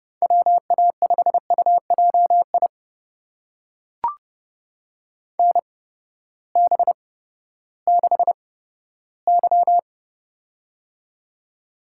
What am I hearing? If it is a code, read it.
Morse code, heard as a keyed tone, about 32 wpm, spaced as for 15 wpm, 700 Hz. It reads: WA5UJI NB6Y
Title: USA Call Signs - 8x Character Spacing - 30wpm